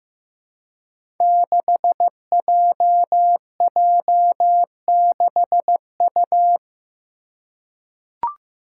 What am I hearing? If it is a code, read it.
Morse code, heard as a keyed tone, about 15 wpm, 700 Hz. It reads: 6JJ6U
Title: Five Challenging Letter-Number Combinations 15wpm